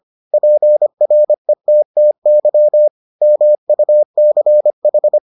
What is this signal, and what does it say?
Morse code, heard as a keyed tone, about 25 wpm, 600 Hz. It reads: PRETTY MUCH